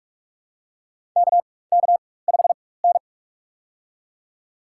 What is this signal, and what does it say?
Morse code, heard as a keyed tone, about 44 wpm, spaced as for 21 wpm, 700 Hz. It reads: KK5N
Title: USA Call Signs - 4 Characters - 4x Character Spacing - 45wpm